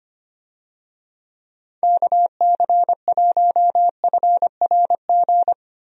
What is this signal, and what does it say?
Morse code, heard as a keyed tone, about 25 wpm, 700 Hz. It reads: KC1FRG
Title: USA Call Signs - 6 Characters - 25wpm